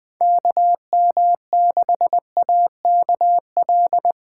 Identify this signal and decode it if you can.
Morse code, heard as a keyed tone, about 20 wpm, 700 Hz. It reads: KM6AKL
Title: Speed Racing - USA 2x3 Call Signs 17wpm